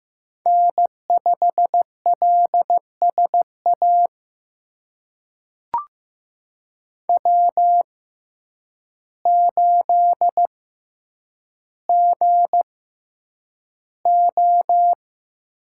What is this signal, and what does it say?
Morse code, heard as a keyed tone, about 15 wpm, spaced as for 6 wpm, 700 Hz. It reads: N5LS AW8GO
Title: USA Call Signs - 6x Character Spacing - 15wpm